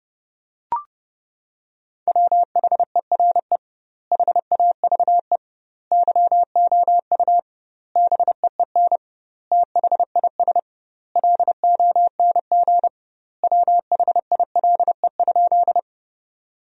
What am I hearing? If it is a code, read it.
Morse code, heard as a keyed tone, about 30 wpm, 700 Hz. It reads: WHERE HAVE YOU BEEN THIS LONG WHILE?
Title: 2x Word-Spacing - Sentences from Top 300 Words - 30wpm